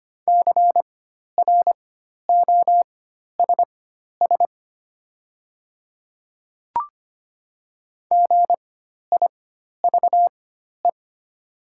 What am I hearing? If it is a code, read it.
Morse code, heard as a keyed tone, about 25 wpm, spaced as for 12 wpm, 700 Hz. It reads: CROSS GIVE
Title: 4x Character Spacing - Top 300 Words - 25wpm